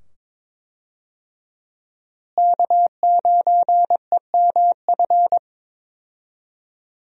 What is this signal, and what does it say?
Morse code, heard as a keyed tone, about 22 wpm, 700 Hz. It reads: K9EMF
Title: USA Call Signs - 5 Characters - Repeated 2x - 22wpm